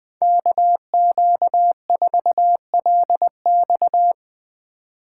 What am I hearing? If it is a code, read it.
Morse code, heard as a keyed tone, about 20 wpm, 700 Hz. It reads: KQ4LX
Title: USA Call Signs - 5 Characters - 3x Character Spacing - 20wpm